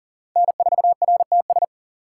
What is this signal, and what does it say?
Morse code, heard as a keyed tone, about 40 wpm, 700 Hz. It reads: N4RTS
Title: USA Call Signs - 5 Characters - 6x Character Spacing - 40wpm